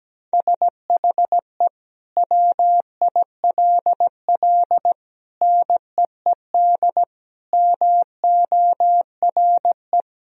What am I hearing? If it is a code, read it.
Morse code, heard as a keyed tone, about 17 wpm, 700 Hz. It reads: SHE WILL NEED MORE
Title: Sentences from Top 1000 Words - Review Part 4 - 17wpm